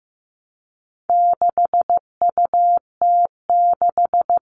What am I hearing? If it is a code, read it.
Morse code, heard as a keyed tone, about 15 wpm, 700 Hz. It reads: 6UT6
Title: All Cut Numbers - 4-Digit 15wpm